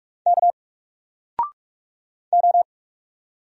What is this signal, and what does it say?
Morse code, heard as a keyed tone, about 46 wpm, 700 Hz. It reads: K O
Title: Single Letter-Number 45wpm